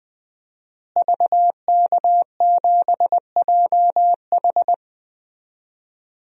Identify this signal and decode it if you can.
Morse code, heard as a keyed tone, about 20 wpm, 700 Hz. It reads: VK7JH
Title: DXCC-100 Call Signs - 5 Characters - Repeated 2x - 20wpm